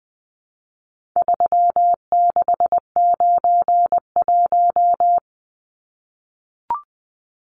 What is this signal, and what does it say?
Morse code, heard as a keyed tone, about 20 wpm, 700 Hz. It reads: 3691
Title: Numbers - 4-Digit 20wpm